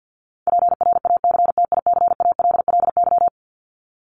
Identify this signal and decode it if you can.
Morse code, heard as a keyed tone, about 50 wpm, 700 Hz. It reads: PRACTICALLY
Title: LY Suffix - Top 100 Words with Base Words 50wpm